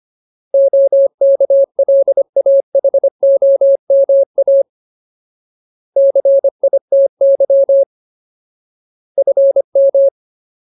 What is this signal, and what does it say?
Morse code, heard as a keyed tone, about 25 wpm, 550 Hz. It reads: OKLAHOMA CITY FM
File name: Ultimate Word List / 25WPM Ultimate